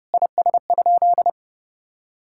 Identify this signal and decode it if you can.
Morse code, heard as a keyed tone, about 30 wpm, 700 Hz. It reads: IS?
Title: Sentences from Top 1000 Words - Review Part 5 - 30wpm